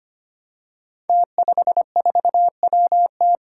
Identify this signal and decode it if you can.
Morse code, heard as a keyed tone, about 25 wpm, 700 Hz. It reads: T54WT
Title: Five Challenging Letter-Number Combinations 25wpm